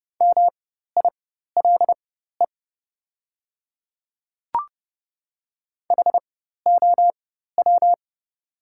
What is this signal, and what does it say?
Morse code, heard as a keyed tone, about 30 wpm, spaced as for 14 wpm, 690 Hz. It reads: MILE HOW
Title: 4x Character Spacing - Top 500 Words - 30wpm